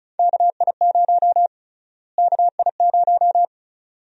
Morse code, heard as a keyed tone, about 36 wpm, 700 Hz. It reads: KI0 KI0
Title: USA Call Sign Prefixes - Repeated 2x - 35wpm